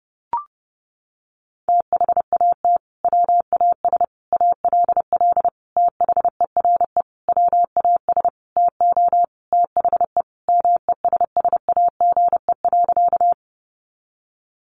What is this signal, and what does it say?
Morse code, heard as a keyed tone, about 30 wpm, 700 Hz. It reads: THAT WAS ALL THERE WAS TO THE MESSAGE.
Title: Sentences from Top 1100 Words 30wpm